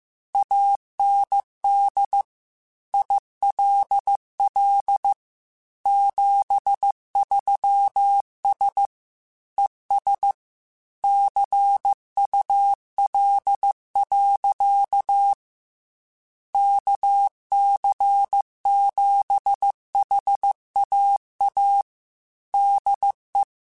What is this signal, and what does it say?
Morse code, heard as a keyed tone, about 15 wpm, 800 Hz. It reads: AND ILL 73S ES CUL. KC7HAA DE